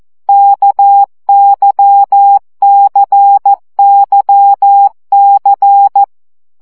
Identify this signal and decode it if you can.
Morse code, heard as a keyed tone, about 14 wpm, 800 Hz. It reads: KYCYC
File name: CW gyakorlas k c y 12 baud